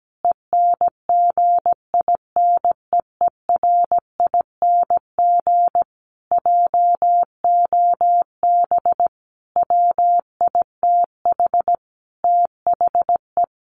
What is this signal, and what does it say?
Morse code, heard as a keyed tone, about 17 wpm, 700 Hz. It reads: ENGINEERING JOB WITH THE
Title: Sentences from Top 1500 Words 17wpm